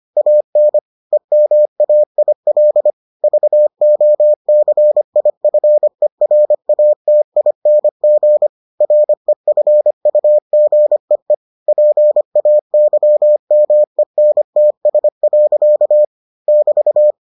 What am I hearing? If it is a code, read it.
Morse code, heard as a keyed tone, about 25 wpm, 600 Hz. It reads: AN EMAIL VOCIFERATING REFUGEE PAYMENTS. <BT>